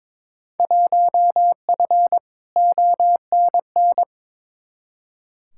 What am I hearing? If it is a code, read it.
Morse code, heard as a keyed tone, about 22 wpm, 700 Hz. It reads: 1F ONN
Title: Field Day Exchanges Encore - 22wpm